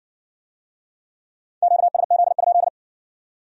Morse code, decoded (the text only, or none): QSZ?